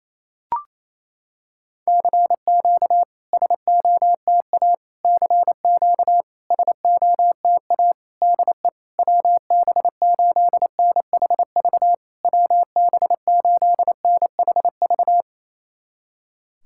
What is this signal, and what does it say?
Morse code, heard as a keyed tone, about 28 wpm, 700 Hz. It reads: CQ SOTA CQ SOTA DE WB8NHV WB8NHV
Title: SOTA QSO Practice 28wpm